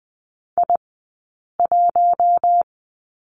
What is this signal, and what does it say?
Morse code, heard as a keyed tone, about 20 wpm, 700 Hz. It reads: I 1
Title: 040 Lesson 8d - Number 1 - Two Characters 20wpm